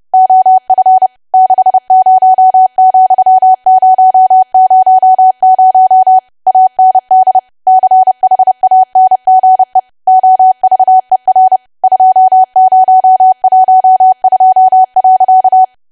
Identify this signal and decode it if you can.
Morse code, heard as a keyed tone, about 30 wpm, 750 Hz. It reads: OF 60,000 AND CHANGE OVER 2012.
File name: cwa / W1AW-Level3 / 30WPM